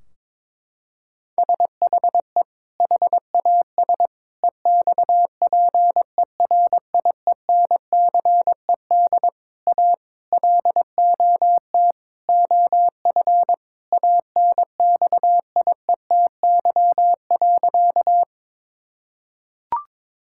Morse code, heard as a keyed tone, about 22 wpm, 700 Hz. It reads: SHE HAS EXPERIENCED A LOT OF ANXIETY.
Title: Sentences from Top 1600 Words 22wpm